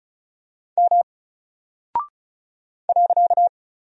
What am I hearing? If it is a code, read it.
Morse code, heard as a keyed tone, about 36 wpm, 700 Hz. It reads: M .
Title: Single Character (Letter:Number:Punctuation:Prosign) Alt 35wpm